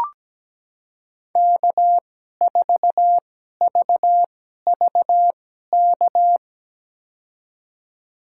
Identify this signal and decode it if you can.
Morse code, heard as a keyed tone, about 17 wpm, spaced as for 12 wpm, 700 Hz. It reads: K4VVK
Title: USA Call Signs - 5 Characters - 2x Character Spacing - 17wpm